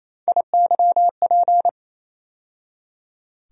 Morse code, heard as a keyed tone, about 28 wpm, 700 Hz. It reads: IYP
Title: USA Call Sign Postfixes - 3 Letters 28wpm